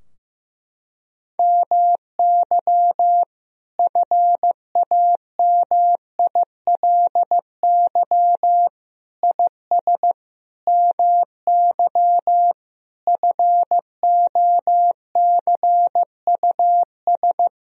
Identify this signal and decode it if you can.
Morse code, heard as a keyed tone, about 15 wpm, 700 Hz. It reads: MY FAMILY IS MY FOCUS
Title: Sentences from Top 1000 Words - Review Part 4 - 15wpm